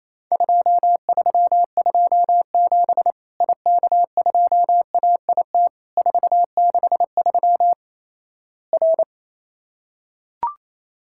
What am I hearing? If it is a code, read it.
Morse code, heard as a keyed tone, about 28 wpm, 700 Hz. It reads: 2327 IK2AIT 463 R
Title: Worked All Europe DX Contest - QTC Exchange - 28wpm